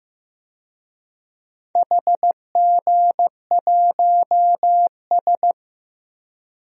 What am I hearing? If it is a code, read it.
Morse code, heard as a keyed tone, about 15 wpm, 700 Hz. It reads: HG1S
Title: DXCC-100 Call Signs - 4 Characters - Repeated 2x - 15wpm